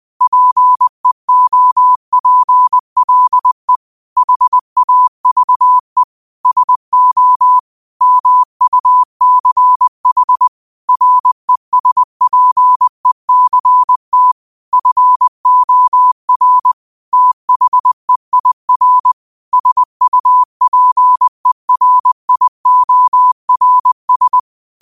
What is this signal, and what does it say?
Morse code, heard as a keyed tone, about 20 wpm, 1000 Hz. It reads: PEOPLE HAVE SO MUCH RESPECT FOR THEIR SUPERIORS
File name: qotdfiles / 20wpm